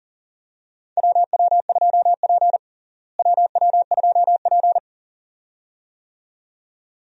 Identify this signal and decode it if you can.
Morse code, heard as a keyed tone, about 40 wpm, 700 Hz. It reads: WW2P WW2P
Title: USA Call Signs - 4 Characters - Repeated 2x - 40wpm